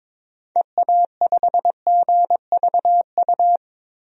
Morse code, heard as a keyed tone, about 22 wpm, 700 Hz. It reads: EA5GVU
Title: DXCC-100 Call Signs - 6 Characters - 22wpm